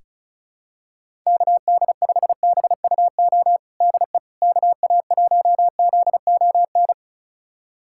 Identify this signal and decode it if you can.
Morse code, heard as a keyed tone, about 36 wpm, 700 Hz. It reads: KD5BUO DE KA1ZON
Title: USA Callsign DE Callsign - 6 Characters - 35wpm